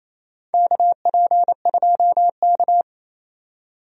Morse code, heard as a keyed tone, about 28 wpm, 700 Hz. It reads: KP2K
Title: USA Call Signs - 4 Characters - 28wpm